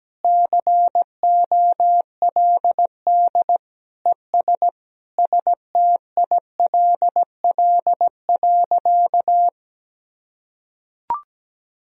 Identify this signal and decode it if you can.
Morse code, heard as a keyed tone, about 17 wpm, 700 Hz. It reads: COLD ES STILL.
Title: Sentences from Top 400 Words and Abbreviations 17wpm